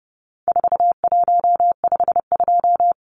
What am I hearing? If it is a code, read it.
Morse code, heard as a keyed tone, about 30 wpm, 700 Hz. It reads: 4152
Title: Numbers - 4-Digit 30wpm